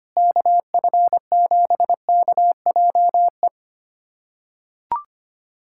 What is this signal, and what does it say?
Morse code, heard as a keyed tone, about 25 wpm, 700 Hz. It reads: KF7KJE
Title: USA Callsign DE Callsign - 6 Characters - 25wpm